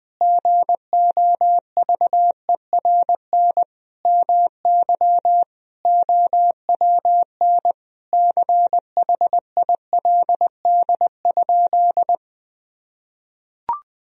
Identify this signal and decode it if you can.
Morse code, heard as a keyed tone, about 21 wpm, 700 Hz. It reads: GOVERN MY OWN CHILD?
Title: Sentences from Top 1000 Words - Review Part 2 - 20wpm